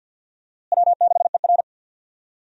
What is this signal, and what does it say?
Morse code, heard as a keyed tone, about 51 wpm, 700 Hz. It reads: W6ER
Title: USA Call Signs - 4 Characters - Repeated 2x - 50wpm